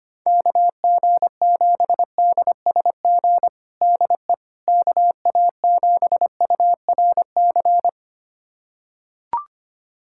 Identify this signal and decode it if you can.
Morse code, heard as a keyed tone, about 25 wpm, 700 Hz. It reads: KG7DSG DE KA7URC
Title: USA Callsign DE Callsign - 6 Characters - 25wpm